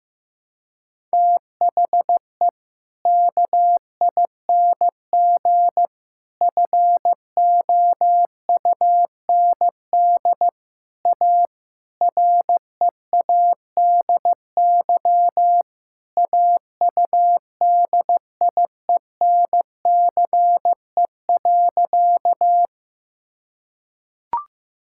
Morse code, heard as a keyed tone, about 15 wpm, 700 Hz. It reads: THE KING FOUND A READY AUDIENCE.
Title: Sentences from Top 1000 Words - Review Part 5 - 15wpm